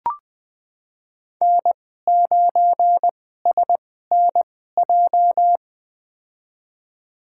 Morse code, heard as a keyed tone, about 20 wpm, spaced as for 14 wpm, 700 Hz. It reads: N9SNJ
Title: USA Call Signs - 5 Characters - 2x Character Spacing - 20wpm